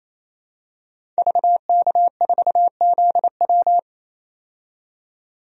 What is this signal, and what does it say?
Morse code, heard as a keyed tone, about 28 wpm, 700 Hz. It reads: VK4ZW
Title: DXCC-100 Call Signs - 5 Characters - 28wpm